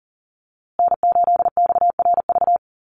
Morse code, heard as a keyed tone, about 40 wpm, 700 Hz. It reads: N8XRV
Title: USA Call Signs Repeated 3x 40wpm